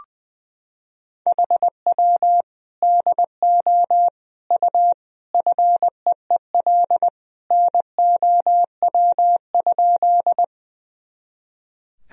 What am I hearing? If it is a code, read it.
Morse code, heard as a keyed tone, about 20 wpm, 700 Hz. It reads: HW DO U FEEL NOW?
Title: Sentences from Top 400 Words and Abbreviations 20wpm